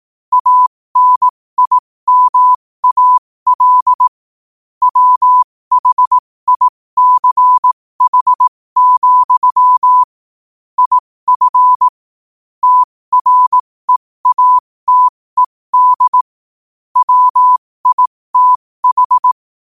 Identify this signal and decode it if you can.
Morse code, heard as a keyed tone, about 18 wpm, spaced as for 15 wpm, 1000 Hz. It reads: ANIMAL WHICH, IF TREATED WITH
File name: qotdfiles / 15wpm